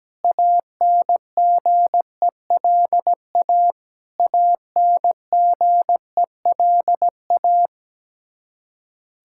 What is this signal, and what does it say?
Morse code, heard as a keyed tone, about 17 wpm, 700 Hz. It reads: ANGELA ANGELA
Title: Practice QSO - Exchange 1 - Beginning - 17wpm